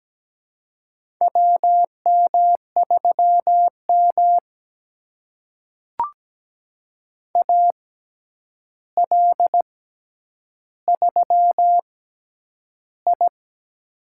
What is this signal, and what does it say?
Morse code, heard as a keyed tone, about 17 wpm, spaced as for 9 wpm, 700 Hz. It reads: WM3M AL3I
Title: USA Call Signs - 4 Characters - 6x Character Spacing - 17wpm